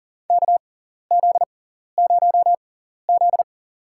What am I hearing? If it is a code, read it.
Morse code, heard as a keyed tone, about 39 wpm, spaced as for 14 wpm, 700 Hz. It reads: KZ0Z
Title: USA Call Signs - 6x Character Spacing - 40wpm